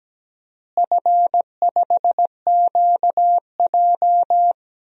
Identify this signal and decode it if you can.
Morse code, heard as a keyed tone, about 17 wpm, 700 Hz. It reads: F5QJ
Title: DXCC-100 Call Signs - 4 Characters - Repeated 3x - 17wpm